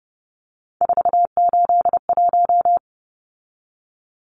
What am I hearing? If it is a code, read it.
Morse code, heard as a keyed tone, about 30 wpm, 700 Hz. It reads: 481
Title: Numbers - 3-Digit 30wpm